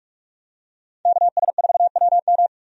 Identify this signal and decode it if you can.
Morse code, heard as a keyed tone, about 43 wpm, 700 Hz. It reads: KS4WM
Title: USA Call Signs - 5 Characters - 45wpm